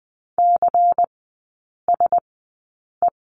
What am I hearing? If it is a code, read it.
Morse code, heard as a keyed tone, about 20 wpm, spaced as for 8 wpm, 700 Hz. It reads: CSE
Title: 138 Lesson 23e - Letter f - Three Characters 20wpm